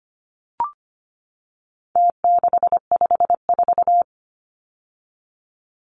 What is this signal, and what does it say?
Morse code, heard as a keyed tone, about 25 wpm, 700 Hz. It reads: T654
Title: Numbers - Contest Style - 4-Digit 25wpm